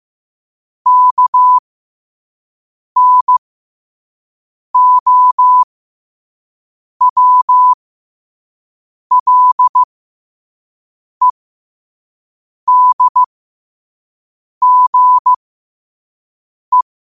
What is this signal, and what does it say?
Morse code, heard as a keyed tone, about 15 wpm, spaced as for 5 wpm, 1000 Hz. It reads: KNOWLEDGE
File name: qotdfiles / 5wpm